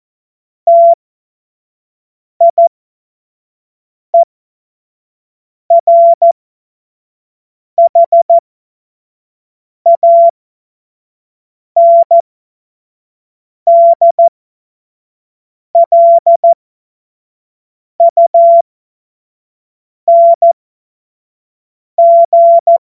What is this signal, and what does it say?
Morse code, heard as a keyed tone, about 14 wpm, spaced as for 5 wpm, 680 Hz. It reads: TIERHANDLUNG
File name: Morsekurs / Witze / Witz 25. Oktober 2022 25 F70